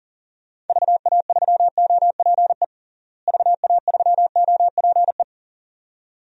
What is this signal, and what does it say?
Morse code, heard as a keyed tone, about 37 wpm, 700 Hz. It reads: VA3OPE VA3OPE
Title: DXCC-100 Call Signs - 6 Characters - Repeated 2x - 40wpm